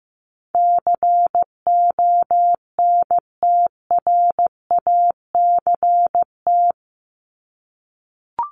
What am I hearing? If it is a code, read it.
Morse code, heard as a keyed tone, about 15 wpm, 700 Hz. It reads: CONTRACT
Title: Top 1251-1500 Words 15wpm